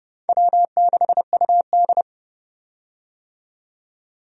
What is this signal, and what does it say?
Morse code, heard as a keyed tone, about 31 wpm, 700 Hz. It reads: W6UD
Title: USA Call Signs - 4 Characters - 30wpm